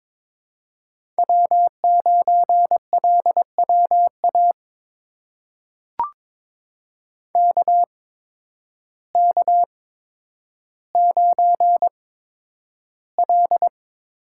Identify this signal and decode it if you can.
Morse code, heard as a keyed tone, about 22 wpm, spaced as for 10 wpm, 700 Hz. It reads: W9LWA KK9L
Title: USA Call Signs - 8x Character Spacing - 22wpm